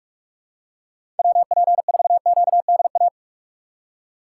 Morse code, heard as a keyed tone, about 42 wpm, 700 Hz. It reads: WP4QDA
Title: USA Call Signs - 6 Characters - 2x Character Spacing - 45wpm